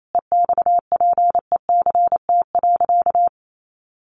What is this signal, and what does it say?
Morse code, heard as a keyed tone, about 28 wpm, 700 Hz. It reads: EXPECT.
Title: Sentences from Top 1100 Words 28wpm